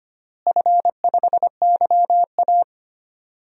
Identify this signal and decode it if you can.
Morse code, heard as a keyed tone, about 25 wpm, 700 Hz. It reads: F5YA
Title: DXCC-100 Call Signs - 4 Characters - Repeated 2x - 25wpm